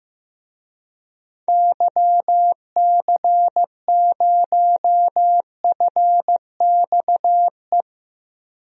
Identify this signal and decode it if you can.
Morse code, heard as a keyed tone, about 15 wpm, 700 Hz. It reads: YC0FXE
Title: DXCC-100 Call Signs - 6 Characters - Repeated 2x - 15wpm